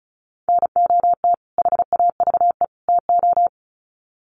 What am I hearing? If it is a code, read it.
Morse code, heard as a keyed tone, about 34 wpm, 700 Hz. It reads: NOT HAVE TO
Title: Sets of 3 Words 35wpm